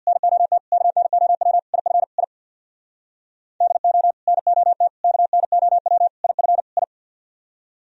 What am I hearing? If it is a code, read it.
Morse code, heard as a keyed tone, about 64 wpm, 700 Hz. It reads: NOT KNOW IF I DO NOT KNOW IF I